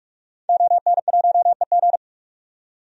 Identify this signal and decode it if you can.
Morse code, heard as a keyed tone, about 42 wpm, 700 Hz. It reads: ON1EG
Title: DXCC-100 Call Signs - 5 Characters - 45wpm